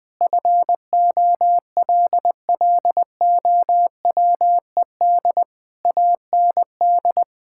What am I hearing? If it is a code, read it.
Morse code, heard as a keyed tone, about 20 wpm, 700 Hz. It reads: FOLLOWED AND
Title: Sentences with 2-Word Phrases - 20wpm